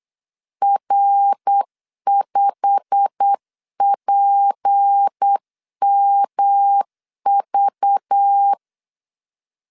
Morse code, heard as a keyed tone, about 8 wpm, 800 Hz. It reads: R5PMV
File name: CW / DET / 1 9 1500C S A 8wpm 5637